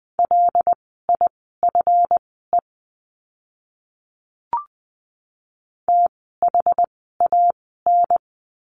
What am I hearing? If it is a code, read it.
Morse code, heard as a keyed tone, about 20 wpm, spaced as for 15 wpm, 700 Hz. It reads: LIFE THAN
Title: 237 Lesson 38f - Letter q - Words 20wpm